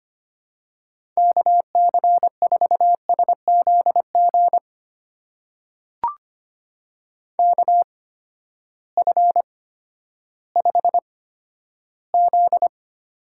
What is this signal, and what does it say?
Morse code, heard as a keyed tone, about 25 wpm, spaced as for 12 wpm, 700 Hz. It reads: KC4SZG KF5Z
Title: USA Call Signs - 6 Characters - 8x Character Spacing - 25wpm